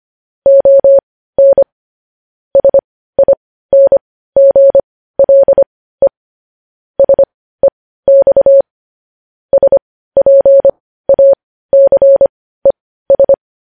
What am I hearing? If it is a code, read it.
Morse code, heard as a keyed tone, about 25 wpm, spaced as for 15 wpm, 550 Hz. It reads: ON SINGLE SEX SPACES
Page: news-headlines-in-morse-code-at-15-wpm